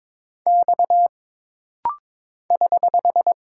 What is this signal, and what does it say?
Morse code, heard as a keyed tone, about 22 wpm, 700 Hz. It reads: X <HH>
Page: This rapid-fire timing is geared to push your ICR (Instant Character Recognition) skills.